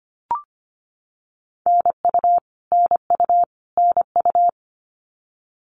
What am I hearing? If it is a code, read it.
Morse code, heard as a keyed tone, about 25 wpm, 700 Hz. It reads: NU NU NU
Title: Canadian Provinces and Territories - Repeated 3x - 25wpm